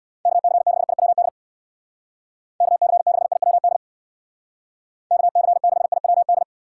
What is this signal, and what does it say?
Morse code, heard as a keyed tone, about 66 wpm, 700 Hz. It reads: KC6IWD KC6IWD KC6IWD